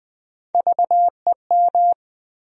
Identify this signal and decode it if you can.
Morse code, heard as a keyed tone, about 20 wpm, 700 Hz. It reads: VEM
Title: USA Call Sign Postfixes - 3 Letters 20wpm